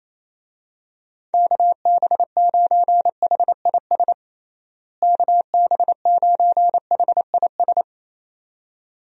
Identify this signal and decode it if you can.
Morse code, heard as a keyed tone, about 27 wpm, 700 Hz. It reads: KB9HIS KB9HIS
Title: USA Call Signs - 6 Characters - Repeated 2x - 28wpm